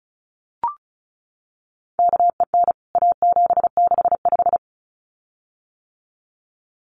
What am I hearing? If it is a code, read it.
Morse code, heard as a keyed tone, about 34 wpm, 700 Hz. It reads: KEN A765
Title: CWT Exchange 35wpm